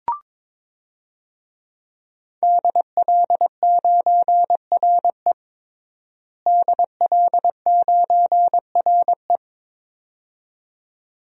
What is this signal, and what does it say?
Morse code, heard as a keyed tone, about 22 wpm, 700 Hz. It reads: DL9RE DL9RE
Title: DXCC-100 Call Signs - 5 Characters - Repeated 2x - 22wpm